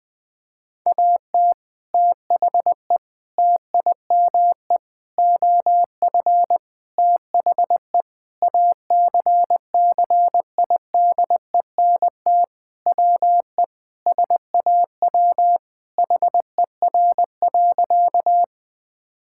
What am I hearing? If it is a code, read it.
Morse code, heard as a keyed tone, about 20 wpm, 700 Hz. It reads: AT THE TIME OF THE ACCIDENT WE SAW HER.
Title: Sentences with 4-Word Phrases - 20wpm